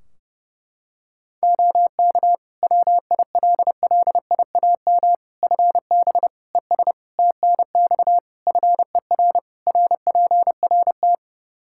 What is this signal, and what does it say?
Morse code, heard as a keyed tone, about 30 wpm, 700 Hz. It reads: OK WILLIAM FB ES TNX FER RPRT
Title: Practice QSO - Exchange 2 - Info - 30wpm